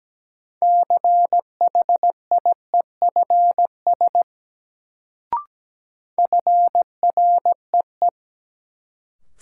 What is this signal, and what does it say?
Morse code, heard as a keyed tone, about 17 wpm, 700 Hz. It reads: CHIEFS FREE
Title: Sets of 5 Words Encore - New Word Drill - 17wpm